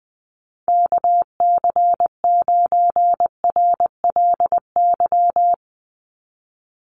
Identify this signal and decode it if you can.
Morse code, heard as a keyed tone, about 20 wpm, 700 Hz. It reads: KC9RLY
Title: USA Call Signs 20wpm